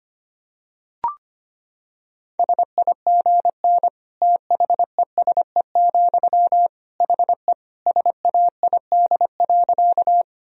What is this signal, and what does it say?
Morse code, heard as a keyed tone, about 25 wpm, 700 Hz. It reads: SIGN THESE, HE SAID.
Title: Sentences from Top 1000 Words - Review Part 1 - 25wpm